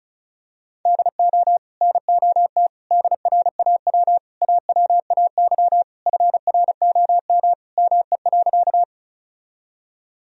Morse code, heard as a keyed tone, about 34 wpm, 700 Hz. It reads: DO NOT DRAW AWAY FROM ME.
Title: Sentences from Top 1000 Words - Review Part 1 - 35wpm